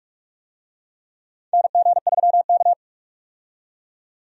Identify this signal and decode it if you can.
Morse code, heard as a keyed tone, about 46 wpm, 700 Hz. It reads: NG3K
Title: USA Call Signs - 4 Characters - 45wpm